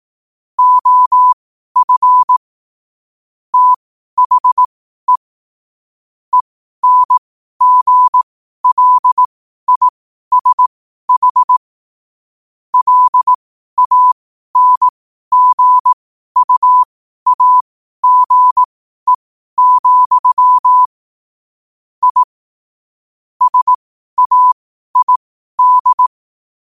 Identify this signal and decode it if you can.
Morse code, heard as a keyed tone, about 18 wpm, spaced as for 12 wpm, 1000 Hz. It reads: OF THE ENGLISH LANGUAGE, I SAID
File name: qotdfiles / 12wpm